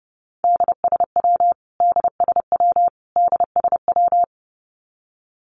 Morse code, heard as a keyed tone, about 30 wpm, 700 Hz. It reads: DSW DSW DSW
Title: Abbreviations Repeated 3x - 30wpm